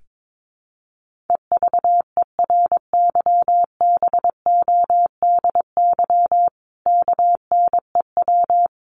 Morse code, heard as a keyed tone, about 22 wpm, 700 Hz. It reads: EVERYBODY KNEW
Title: Sentences from Top 1400 Words 22wpm